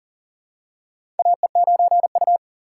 Morse code, heard as a keyed tone, about 40 wpm, 700 Hz. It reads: AE9U
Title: USA Call Signs - 4 Characters - 4x Character Spacing - 40wpm